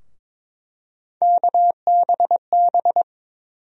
Morse code, heard as a keyed tone, about 22 wpm, 700 Hz. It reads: KBB
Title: USA Call Sign Postfixes - 3 Letters 22wpm